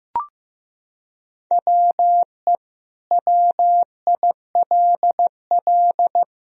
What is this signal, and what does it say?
Morse code, heard as a keyed tone, about 15 wpm, 700 Hz. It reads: WE WILL
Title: Sentences from Top 1000 Words - Review Part 4 - 15wpm